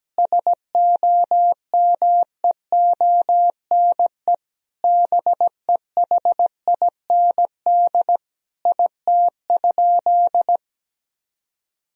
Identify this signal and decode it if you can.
Morse code, heard as a keyed tone, about 17 wpm, 700 Hz. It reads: SOMEONE BEHIND IT?
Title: Sentences from Top 1000 Words - Review Part 4 - 17wpm